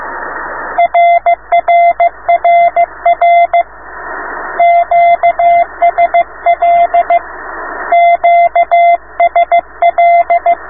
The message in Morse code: RRRR QSL QSL